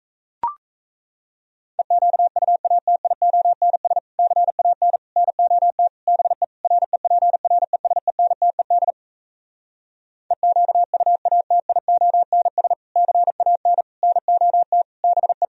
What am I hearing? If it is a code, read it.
Morse code, heard as a keyed tone, about 38 wpm, 700 Hz. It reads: EQUATIONS CAN NOT BE REPRESENTED EQUATIONS CAN NOT BE